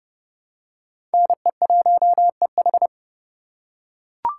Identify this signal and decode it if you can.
Morse code, heard as a keyed tone, about 30 wpm, 700 Hz. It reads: NE1EH
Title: USA Call Signs - 5 Characters - 4x Character Spacing - 30wpm